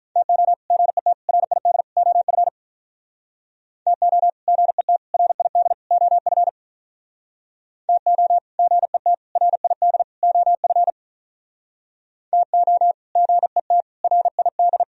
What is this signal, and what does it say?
Morse code, heard as a keyed tone, about 44 wpm, 700 Hz. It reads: TO GET RID OF TO GET RID OF TO GET RID OF TO GET RID